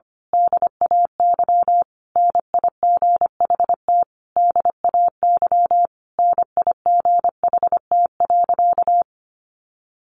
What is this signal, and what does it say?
Morse code, heard as a keyed tone, about 25 wpm, 700 Hz. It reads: DAY NIGHT DAY NIGHT.
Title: A Sound Of Thunder 25wpm